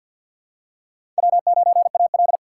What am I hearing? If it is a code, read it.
Morse code, heard as a keyed tone, about 47 wpm, 700 Hz. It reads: W9AL
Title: USA Call Signs - 4 Characters - 6x Character Spacing - 50wpm